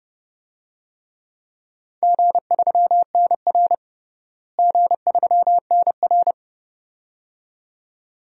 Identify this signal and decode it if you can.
Morse code, heard as a keyed tone, about 32 wpm, 700 Hz. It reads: G3NR G3NR
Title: DXCC-100 Call Signs - 4 Characters - Repeated 2x - 30wpm